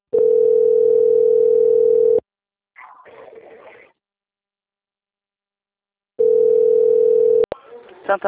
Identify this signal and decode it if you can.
Ringback tone (North America), 440 + 480 Hz, with speech.
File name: out- - - - - .8712